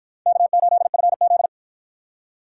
Morse code, heard as a keyed tone, about 50 wpm, 700 Hz. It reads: K8FZ